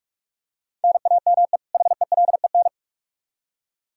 Morse code, heard as a keyed tone, about 41 wpm, 700 Hz. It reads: NAME HELEN
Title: Practice QSO - Exchange 1 - Beginning - 45wpm